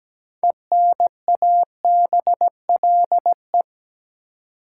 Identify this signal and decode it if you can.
Morse code, heard as a keyed tone, about 17 wpm, 700 Hz. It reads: ENABLE
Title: Sentences from Top 1500 Words - New Word Drill - 17wpm